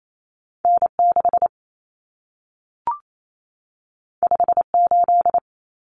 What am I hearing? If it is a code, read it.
Morse code, heard as a keyed tone, about 28 wpm, 700 Hz. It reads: N6 58
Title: Common Cut Numbers - 2-Digit 28wpm